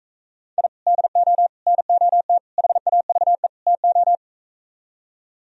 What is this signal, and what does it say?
Morse code, heard as a keyed tone, about 45 wpm, 700 Hz. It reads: I DO NOT HAVE TO